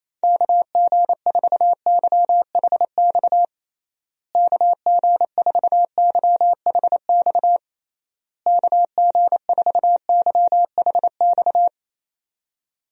Morse code, heard as a keyed tone, about 28 wpm, 700 Hz. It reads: KG4YHX KG4YHX KG4YHX
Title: USA Call Signs - 6 Characters - Repeated 3x - 28wpm